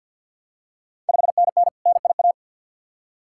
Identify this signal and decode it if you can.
Morse code, heard as a keyed tone, about 49 wpm, 700 Hz. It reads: 5NN NIA
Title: Ontario QSO Party Exchange 50wpm